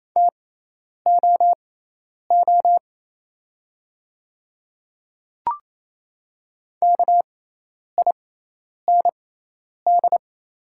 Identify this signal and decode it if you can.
Morse code, heard as a keyed tone, about 27 wpm, spaced as for 10 wpm, 690 Hz. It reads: TOO KIND
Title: 6x Character Spacing - Top 300 Words - 28wpm